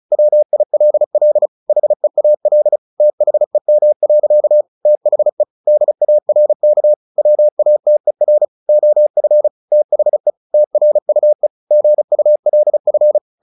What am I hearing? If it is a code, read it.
Morse code, heard as a keyed tone, about 34 wpm, 600 Hz. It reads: WILL HEAL THEM. THE DARK WATER OF THE TRUE GULF
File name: the old man and the sea 2 35wpm